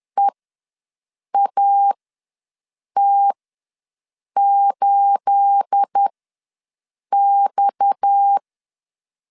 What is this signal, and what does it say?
Morse code, heard as a keyed tone, about 10 wpm, spaced as for 6 wpm, 800 Hz. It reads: EAT8X
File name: LEZIONI / 9 28 ws 10wpm 4124